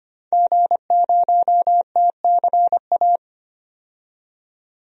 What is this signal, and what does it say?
Morse code, heard as a keyed tone, about 25 wpm, 700 Hz. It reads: G0TCA
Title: DXCC-100 Call Signs - 5 Characters - 25wpm